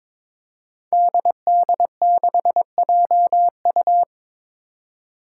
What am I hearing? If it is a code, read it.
Morse code, heard as a keyed tone, about 22 wpm, 700 Hz. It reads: DD6JU
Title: Five Challenging Letter-Number Combinations 22wpm